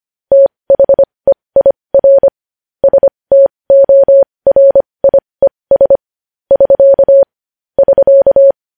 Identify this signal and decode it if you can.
Morse code, heard as a keyed tone, about 25 wpm, spaced as for 20 wpm, 550 Hz. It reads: THEIR STORIES <SK> <SK>
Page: news-headlines-in-morse-code-at-20-wpm